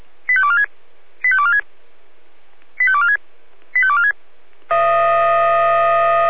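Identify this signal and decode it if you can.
Five-tone selective call (ZVEI), heard as quick runs of five tones, tones of about 70 ms. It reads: 85246 85246 85246 85246
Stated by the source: ZVEI five-tone selective call received on SDR.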